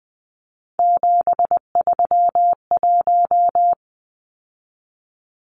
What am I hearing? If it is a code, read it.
Morse code, heard as a keyed tone, about 20 wpm, 700 Hz. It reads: 731
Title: Numbers - 3-Digit 20wpm